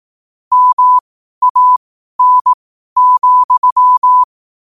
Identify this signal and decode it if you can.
Morse code, heard as a keyed tone, about 18 wpm, spaced as for 12 wpm, 1000 Hz. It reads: MAN,
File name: qotdfiles / 12wpm